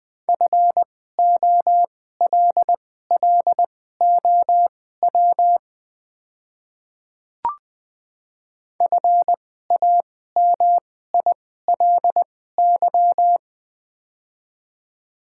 Morse code, heard as a keyed tone, about 20 wpm, spaced as for 15 wpm, 700 Hz. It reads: FOLLOW FAMILY
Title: 2x Character Spacing - Top 500 Words - 20wpm